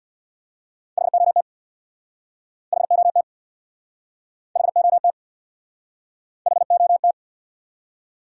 Morse code, heard as a keyed tone, about 71 wpm, 700 Hz. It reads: HOT HOT HOT HOT